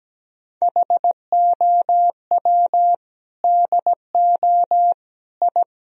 Morse code, heard as a keyed tone, about 17 wpm, 700 Hz. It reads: HOW DO I
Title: Sentences from Top 1000 Words - Review Part 2 - 17wpm